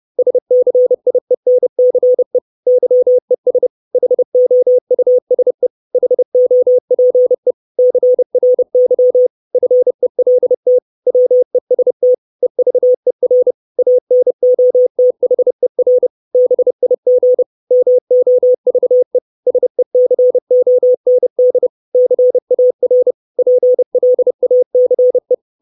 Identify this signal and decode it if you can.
Morse code, heard as a keyed tone, about 30 wpm, 500 Hz. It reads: SCIENCE YES HOUSE HOPE CRY FELT WEST EVER ANOTHER BIG MOVE SECOND CAR PLACE
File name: 632 Words 30WPM